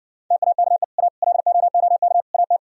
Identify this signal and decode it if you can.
Morse code, heard as a keyed tone, about 69 wpm, 700 Hz. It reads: TAKE A LOOK AT